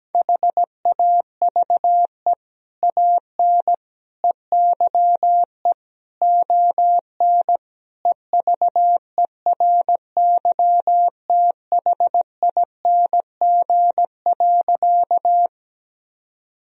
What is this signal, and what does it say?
Morse code, heard as a keyed tone, about 17 wpm, 700 Hz. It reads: HAVE AN EYE ON EVERYTHING.
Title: Sentences from Top 1000 Words - Review Part 1 - 17wpm